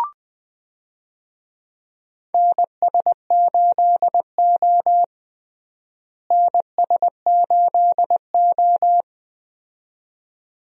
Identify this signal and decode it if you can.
Morse code, heard as a keyed tone, about 20 wpm, 700 Hz. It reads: NS8O NS8O
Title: USA Call Signs - 4 Characters - Repeated 2x - 20wpm